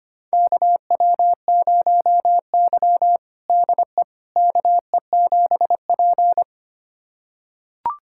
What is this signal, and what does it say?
Morse code, heard as a keyed tone, about 25 wpm, 700 Hz. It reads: KW0Y DE KE7P
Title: USA Callsign DE Callsign - 4 Characters - 25wpm